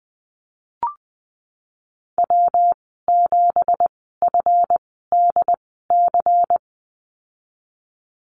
Morse code, heard as a keyed tone, about 20 wpm, spaced as for 14 wpm, 700 Hz. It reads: W7FDC
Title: 193 Lesson 31g - Letter v - Call Signs 20wpm